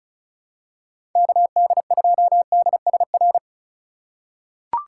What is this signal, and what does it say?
Morse code, heard as a keyed tone, about 36 wpm, 700 Hz. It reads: KD2DSR
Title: USA Call Signs - 6 Characters - 3x Character Spacing - 35wpm